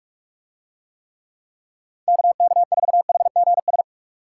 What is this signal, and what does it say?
Morse code, heard as a keyed tone, about 44 wpm, 700 Hz. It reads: KK4HGS